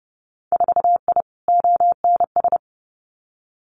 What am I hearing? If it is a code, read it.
Morse code, heard as a keyed tone, about 32 wpm, 700 Hz. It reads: 4I ONS
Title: Winter Field Day Exchanges Encore 30wpm